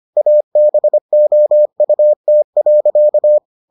Morse code, heard as a keyed tone, about 25 wpm, 600 Hz. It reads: ABOUT.